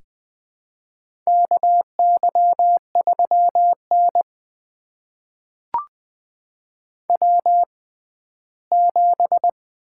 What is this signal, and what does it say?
Morse code, heard as a keyed tone, about 20 wpm, spaced as for 14 wpm, 700 Hz. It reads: KY3N W7
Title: USA Call Signs - 4 Characters - 6x Character Spacing - 20wpm